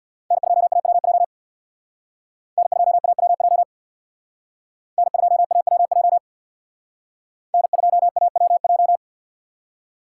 Morse code, heard as a keyed tone, about 62 wpm, 700 Hz. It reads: N2AWJ N2AWJ N2AWJ N2AWJ
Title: Speed Racing - USA 1x3 Call Signs 50wpm